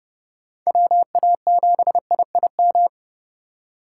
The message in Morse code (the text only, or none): WA7IIM